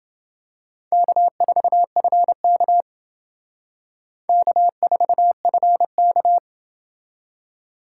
Morse code, heard as a keyed tone, about 27 wpm, 700 Hz. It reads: K4FK K4FK